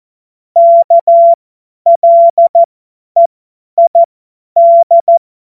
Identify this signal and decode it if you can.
Morse code, heard as a keyed tone, about 14 wpm, spaced as for 10 wpm, 680 Hz. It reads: KLEID
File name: Morsekurs / Witze / Witz 19 Dezember 2023 50 F70